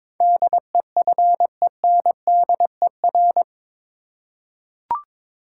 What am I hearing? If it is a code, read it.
Morse code, heard as a keyed tone, about 22 wpm, 700 Hz. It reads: DEFENDER
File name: practice / 22wpm / Eight Letter Words - 22wpm